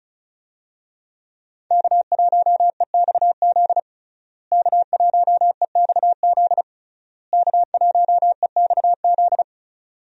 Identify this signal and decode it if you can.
Morse code, heard as a keyed tone, about 36 wpm, 700 Hz. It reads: K1EXZ K1EXZ K1EXZ
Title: USA Call Signs - 5 Characters - Repeated 3x - 35wpm